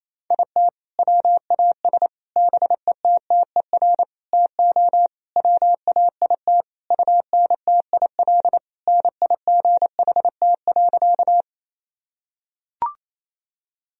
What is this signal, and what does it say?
Morse code, heard as a keyed tone, about 28 wpm, 700 Hz. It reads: IT WAS BETTER TO WAIT UNTIL NIGHT.
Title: Sentences from Top 1000 Words - Review Part 1 - 28wpm